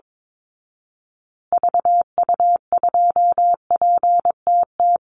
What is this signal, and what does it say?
Morse code, heard as a keyed tone, about 22 wpm, 700 Hz. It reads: VU2PTT
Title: Call Signs Repeated 3x - 22wpm